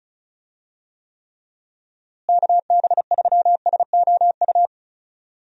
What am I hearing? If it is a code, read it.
Morse code, heard as a keyed tone, about 36 wpm, 700 Hz. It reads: KB3SOU
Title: USA Call Signs - 6 Characters - Repeated 3x - 35wpm